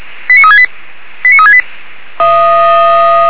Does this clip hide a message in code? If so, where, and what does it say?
Yes: five-tone selective call (ZVEI) from 0.3 to 1.6 s, heard as quick runs of five tones, tones of about 70 ms. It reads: 89368 89368